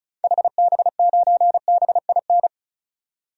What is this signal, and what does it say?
Morse code, heard as a keyed tone, about 36 wpm, 700 Hz. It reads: HB9BIN
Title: DXCC-100 Call Signs - 6 Characters - 35wpm